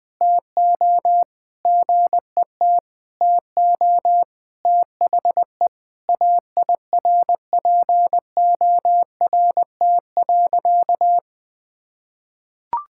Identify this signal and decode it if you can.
Morse code, heard as a keyed tone, about 21 wpm, 700 Hz. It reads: TO GET TO THE AIRPORT.
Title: Sentences with 3-Word Phrases - 20wpm